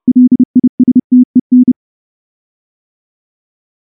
Morse code, heard as a keyed tone, about 28 wpm, 250 Hz. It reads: LISTEN
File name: morse / morse-training-30WPM